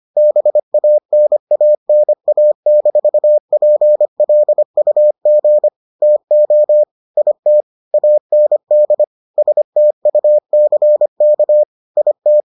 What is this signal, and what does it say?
Morse code, heard as a keyed tone, about 25 wpm, 600 Hz. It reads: BANANA-PLUG TO IT AND STUCK IT